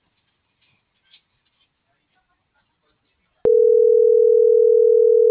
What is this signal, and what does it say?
Ringback tone (North America), 440 + 480 Hz.
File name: CENTRAL TERRESTRE / out claudiacruz